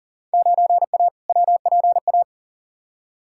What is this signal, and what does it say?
Morse code, heard as a keyed tone, about 40 wpm, 700 Hz. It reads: 9A WPA
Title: Field Day Exchanges Encore - 40wpm